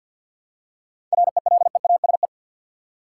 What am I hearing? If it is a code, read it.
Morse code, heard as a keyed tone, about 53 wpm, 700 Hz. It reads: RELEASE
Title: Seven Letter Words - 50wpm